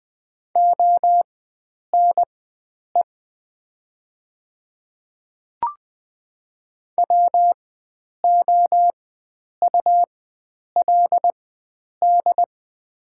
Morse code, heard as a keyed tone, about 20 wpm, spaced as for 9 wpm, 700 Hz. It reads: ONE WOULD
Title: 4x Character Spacing - Top 100 Words - 20wpm